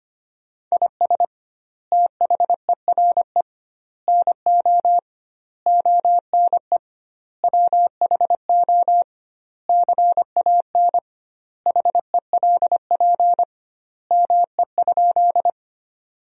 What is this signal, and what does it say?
Morse code, heard as a keyed tone, about 25 wpm, 700 Hz. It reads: IS THERE NO ONE WHO CAN HELP ME?
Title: 2x Word-Spacing - Sentences from Top 200 Words - 25wpm